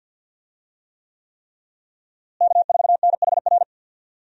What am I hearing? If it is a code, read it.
Morse code, heard as a keyed tone, about 48 wpm, 700 Hz. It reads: K4NHR
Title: USA Call Signs - 5 Characters - Repeated 3x - 50wpm